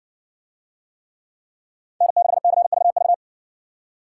Morse code, heard as a keyed tone, about 59 wpm, 700 Hz. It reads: N6CVV